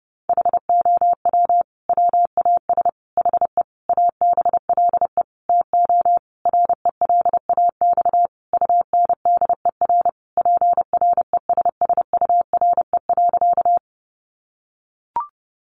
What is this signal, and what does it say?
Morse code, heard as a keyed tone, about 30 wpm, 700 Hz. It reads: HOW WAS HE ABLE TO RELAX UNDER PRESSURE.
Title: Sentences from Top 1400 Words 30wpm